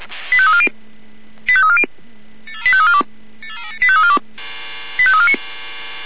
Five-tone selective call (ZVEI), heard as quick runs of five tones, tones of about 70 ms. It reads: 85340 85340 85342 85349